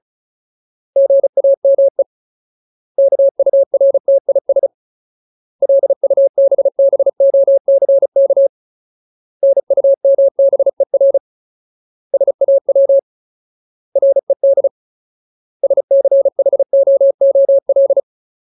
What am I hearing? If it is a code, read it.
Morse code, heard as a keyed tone, about 34 wpm, 550 Hz. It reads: GAME KURTIS LUBBOCK NUMBER SAW RED SCHOOL
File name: Ultimate Word List / 35WPM Ultimate